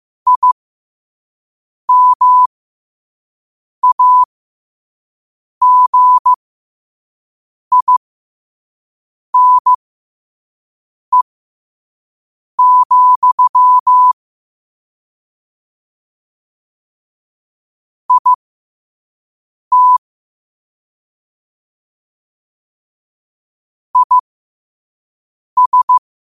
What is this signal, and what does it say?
Morse code, heard as a keyed tone, about 15 wpm, spaced as for 5 wpm, 1000 Hz. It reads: IMAGINE, IT IS
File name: qotdfiles / 5wpm